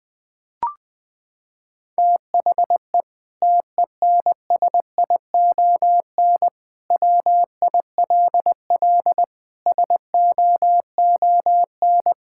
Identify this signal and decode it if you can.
Morse code, heard as a keyed tone, about 20 wpm, 700 Hz. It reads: THE TENSION WILL SOON
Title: Sentences from Top 1600 Words 20wpm